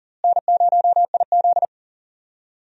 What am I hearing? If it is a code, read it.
Morse code, heard as a keyed tone, about 38 wpm, 700 Hz. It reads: N0IZ
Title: USA Call Signs - 4 Characters - Repeated 3x - 40wpm